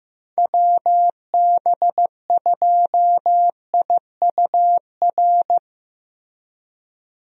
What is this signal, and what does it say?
Morse code, heard as a keyed tone, about 15 wpm, 700 Hz. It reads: WB2IUR
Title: USA Call Signs - 6 Characters - 15wpm